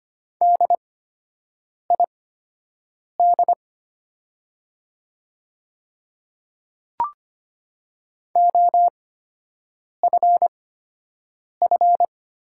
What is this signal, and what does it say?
Morse code, heard as a keyed tone, about 25 wpm, spaced as for 7 wpm, 700 Hz. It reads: DID OFF
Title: 8x Character Spacing - Top 200 Words - 25wpm